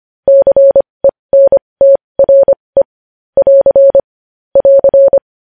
Morse code, heard as a keyed tone, about 25 wpm, spaced as for 20 wpm, 550 Hz. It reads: CENTRE <AR> <AR>
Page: news-headlines-in-morse-code-at-20-wpm